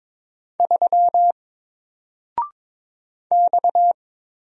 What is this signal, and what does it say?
Morse code, heard as a keyed tone, about 22 wpm, 700 Hz. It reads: 3 X
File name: Single-Letter-Number-22wpm